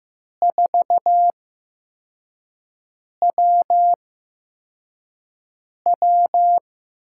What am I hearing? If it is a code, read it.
Morse code, heard as a keyed tone, about 15 wpm, spaced as for 4 wpm, 700 Hz. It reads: 4WW
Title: USA Call Signs - 4 Characters - 8x Character Spacing - 15wpm